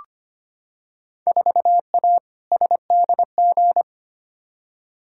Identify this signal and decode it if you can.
Morse code, heard as a keyed tone, about 25 wpm, 700 Hz. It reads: 4A SDG
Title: Field Day Exchanges Encore - 25wpm